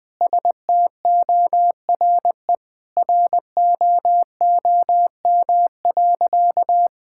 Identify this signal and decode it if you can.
Morse code, heard as a keyed tone, about 20 wpm, 700 Hz. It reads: STORE ROOM.
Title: Sentences from Top 1000 Words - Review Part 5 - 20wpm